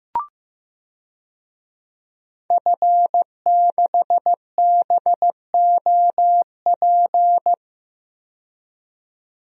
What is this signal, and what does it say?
Morse code, heard as a keyed tone, about 16 wpm, 700 Hz. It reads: F6BOP
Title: DXCC-100 Call Signs - 5 Characters - 15wpm